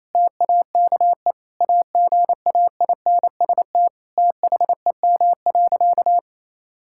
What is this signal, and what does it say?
Morse code, heard as a keyed tone, about 28 wpm, 700 Hz. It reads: TAKE AGAINST THEM.
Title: Sentences from Top 1000 Words - Review Part 6 - 28wpm